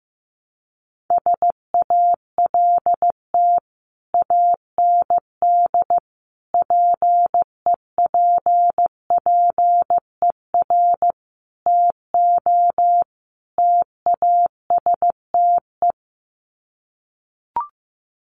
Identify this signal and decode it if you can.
Morse code, heard as a keyed tone, about 15 wpm, 700 Hz. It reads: SALT AND PEPPER TO TASTE
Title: Sets of 5 Words 15wpm